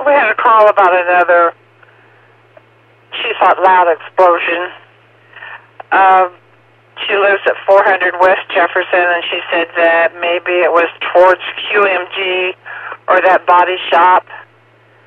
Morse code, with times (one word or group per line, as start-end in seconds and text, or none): none